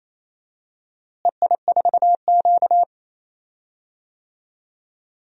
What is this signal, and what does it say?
Morse code, heard as a keyed tone, about 27 wpm, 700 Hz. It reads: EI4Q
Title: DXCC-100 Call Signs - 4 Characters - Repeated 3x - 28wpm